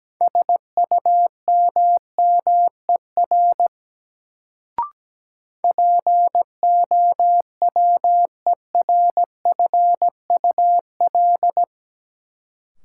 Morse code, heard as a keyed tone, about 17 wpm, 700 Hz. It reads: SUMMER POWERFUL
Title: Sentences from Top 1000 Words - Review Part 6 - New Word Drill - 17wpm